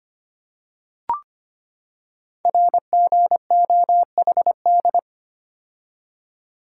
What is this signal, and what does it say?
Morse code, heard as a keyed tone, about 25 wpm, 700 Hz. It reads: RGOHD
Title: Code Groups - 5 Letters - 25wpm